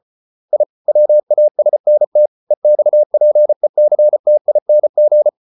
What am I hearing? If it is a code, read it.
Morse code, heard as a keyed tone, about 34 wpm, 600 Hz. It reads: I WASNT EXPECTING